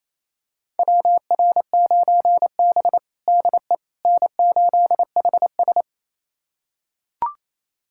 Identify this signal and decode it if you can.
Morse code, heard as a keyed tone, about 27 wpm, 700 Hz. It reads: WR9B DE N8HS
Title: USA Callsign DE Callsign - 4 Characters - 28wpm